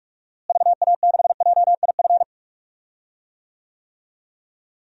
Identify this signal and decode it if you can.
Morse code, heard as a keyed tone, about 46 wpm, 700 Hz. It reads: VA6JIF
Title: DXCC-100 Call Signs - 6 Characters - Repeated 3x - 45wpm